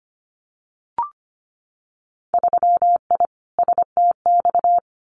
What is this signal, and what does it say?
Morse code, heard as a keyed tone, about 25 wpm, 700 Hz. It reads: 3I STX
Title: Winter Field Day Exchanges 25wpm